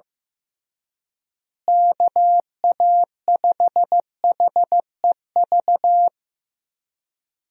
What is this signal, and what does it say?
Morse code, heard as a keyed tone, about 15 wpm, 700 Hz. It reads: KA5HEV
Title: USA Call Signs - 6 Characters - Repeated 3x - 15wpm